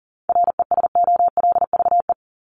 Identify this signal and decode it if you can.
Morse code, heard as a keyed tone, about 40 wpm, 700 Hz. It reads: RESOLVE
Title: Top 1751-2000 Words 40wpm